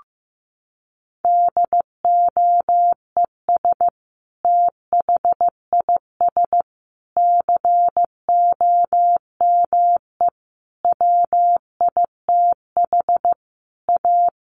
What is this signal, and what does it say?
Morse code, heard as a keyed tone, about 15 wpm, 700 Hz. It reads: DOES THIS COME WITH A
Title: Sentences from Top 1400 Words 15wpm